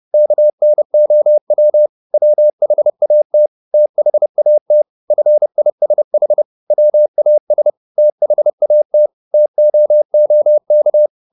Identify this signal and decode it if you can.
Morse code, heard as a keyed tone, about 30 wpm, 600 Hz. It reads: KNOW WHAT THAT FISH WAS THAT TOOK
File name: the old man and the sea / the old man and the sea 1 30wpm